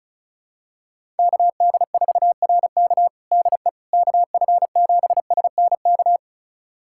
Morse code, heard as a keyed tone, about 34 wpm, 700 Hz. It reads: KD4RK DE KF7SNK
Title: USA Callsign DE Callsign 35wpm